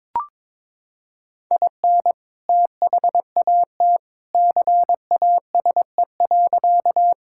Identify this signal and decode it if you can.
Morse code, heard as a keyed tone, about 22 wpm, 700 Hz. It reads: IN THAT CASE.
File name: practice / 22wpm / Sentences with 2-Word Phrases - 22wpm